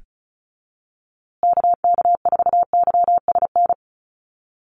Morse code, heard as a keyed tone, about 34 wpm, 700 Hz. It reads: KK4YSN
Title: USA Call Signs Repeated 2x 35wpm